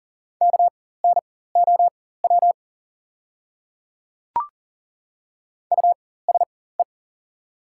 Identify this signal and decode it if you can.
Morse code, heard as a keyed tone, about 40 wpm, spaced as for 19 wpm, 690 Hz. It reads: KNOW USE
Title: 4x Character Spacing - Top 100 Words - 40wpm